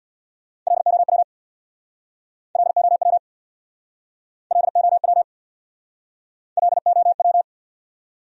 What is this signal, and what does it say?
Morse code, heard as a keyed tone, about 64 wpm, 700 Hz. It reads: LOW LOW LOW LOW